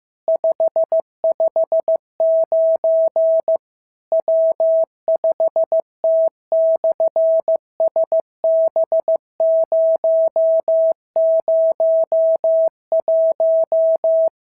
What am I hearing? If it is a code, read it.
Morse code, heard as a keyed tone, about 14 wpm, 650 Hz. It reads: 559 W5T/SB001
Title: SOTA QSO Practice 15wpm